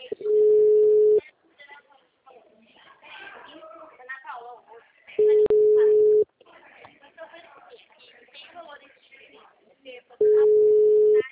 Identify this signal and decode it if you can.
Ringback tone (Europe, 425 Hz), with speech.
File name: out- -2017- - - .89